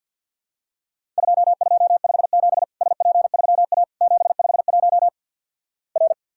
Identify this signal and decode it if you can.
Morse code, heard as a keyed tone, about 50 wpm, 700 Hz. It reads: 2257 SP3A 751 R
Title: Worked All Europe DX Contest - QTC Exchange with Repeats - 50wpm